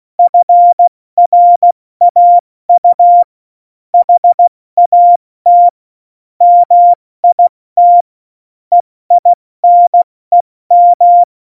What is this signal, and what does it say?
Morse code, heard as a keyed tone, about 16 wpm, 700 Hz. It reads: FRAU HAT MIT EINEM
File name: Morsekurs / Witze / Witz 01. September 2020, 70 F80